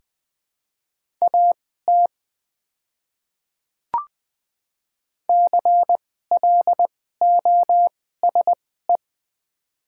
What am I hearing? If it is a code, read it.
Morse code, heard as a keyed tone, about 20 wpm, spaced as for 14 wpm, 700 Hz. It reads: AT CLOSE
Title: 2x Character Spacing - Top 300 Words - 20wpm